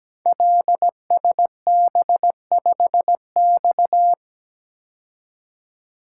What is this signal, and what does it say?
Morse code, heard as a keyed tone, about 17 wpm, 700 Hz. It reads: LSB5X
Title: Code Groups - 5 Letter:Numbers - 17wpm